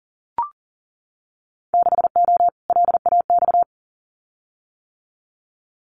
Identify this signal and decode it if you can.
Morse code, heard as a keyed tone, about 42 wpm, 700 Hz. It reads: 6O LAX
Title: Winter Field Day Exchanges Encore 40wpm